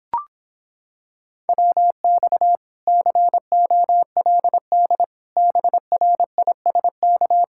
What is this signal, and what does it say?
Morse code, heard as a keyed tone, about 26 wpm, 700 Hz. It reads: WX COLD BRISK